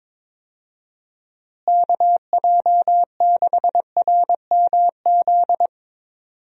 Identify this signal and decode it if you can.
Morse code, heard as a keyed tone, about 22 wpm, 700 Hz. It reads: KJ6RMZ
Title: USA Call Signs - 6 Characters - Repeated 3x - 22wpm